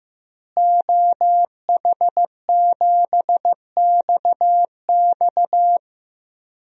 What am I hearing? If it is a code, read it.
Morse code, heard as a keyed tone, about 15 wpm, 700 Hz. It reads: OH7XX
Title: DXCC-100 Call Signs - 5 Characters - Repeated 2x - 15wpm